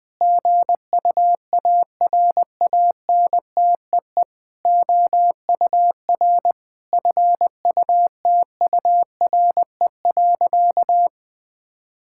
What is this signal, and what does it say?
Morse code, heard as a keyed tone, about 20 wpm, 700 Hz. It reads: GUARANTEE OUR FUTURE.
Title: Sentences from Top 1600 Words 20wpm